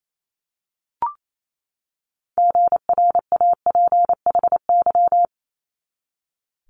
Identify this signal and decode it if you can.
Morse code, heard as a keyed tone, about 28 wpm, 700 Hz. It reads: GRAPHY
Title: Multi-letter Phonemes 28wpm